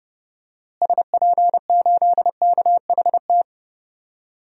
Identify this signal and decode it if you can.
Morse code, heard as a keyed tone, about 30 wpm, 700 Hz. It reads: SP8KHT
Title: DXCC-100 Call Signs - 6 Characters - Repeated 2x - 30wpm